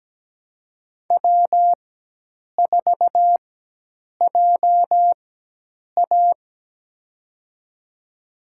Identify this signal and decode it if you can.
Morse code, heard as a keyed tone, about 17 wpm, spaced as for 8 wpm, 700 Hz. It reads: W4JA
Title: USA Call Signs - 4 Characters - 4x Character Spacing - 17wpm